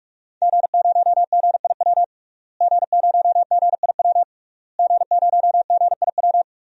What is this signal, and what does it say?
Morse code, heard as a keyed tone, about 46 wpm, 700 Hz. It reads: G0GIW G0GIW G0GIW
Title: DXCC-100 Call Signs - 5 Characters - Repeated 3x - 45wpm